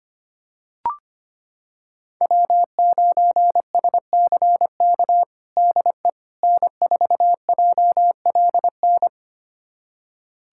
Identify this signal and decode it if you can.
Morse code, heard as a keyed tone, about 25 wpm, 700 Hz. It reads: W9SCK DE N4JLN
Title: USA Callsign DE Callsign - 5 Characters - 25wpm